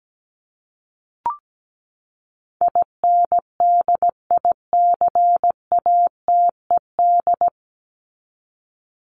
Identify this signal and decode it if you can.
Morse code, heard as a keyed tone, about 17 wpm, 700 Hz. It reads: INDICATED
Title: ED Suffix - Top 100 Words with Base Words 17wpm